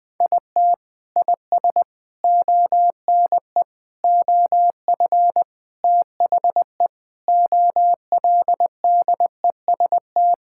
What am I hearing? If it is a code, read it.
Morse code, heard as a keyed tone, about 21 wpm, 700 Hz. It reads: IT IS ONE OF THE OLDEST
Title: Sentences with 5-Word Phrases - 20wpm